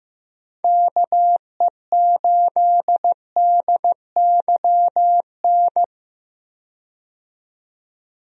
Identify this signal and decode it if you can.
Morse code, heard as a keyed tone, about 15 wpm, 700 Hz. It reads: KE8DYN
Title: USA Call Signs - 6 Characters - 15wpm